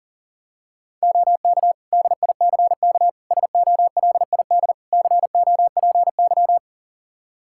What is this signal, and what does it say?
Morse code, heard as a keyed tone, about 40 wpm, 700 Hz. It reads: OK DICK SOLID COPY
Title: Practice QSO - Exchange 3 - More Info - 40wpm